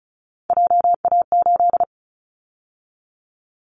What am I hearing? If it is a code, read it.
Morse code, heard as a keyed tone, about 36 wpm, 690 Hz. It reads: JA8
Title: Callsign Prefixes for Top 50 DXCC - 35wpm